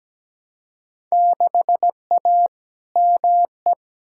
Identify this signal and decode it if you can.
Morse code, heard as a keyed tone, about 17 wpm, 700 Hz. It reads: 6A ME
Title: Field Day Exchanges - 17wpm